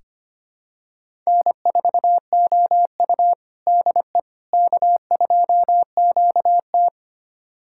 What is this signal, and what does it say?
Morse code, heard as a keyed tone, about 25 wpm, 700 Hz. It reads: N4OU DE K2QT
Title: USA Callsign DE Callsign - 4 Characters - 25wpm